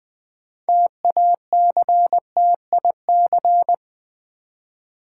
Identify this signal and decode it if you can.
Morse code, heard as a keyed tone, about 20 wpm, 700 Hz. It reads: TACTIC
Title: Build A Word - 3 Letters At A Time - 8-Letter Words - 20wpm